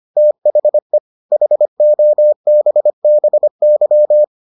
Morse code, heard as a keyed tone, about 25 wpm, 600 Hz. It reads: THE HOBBY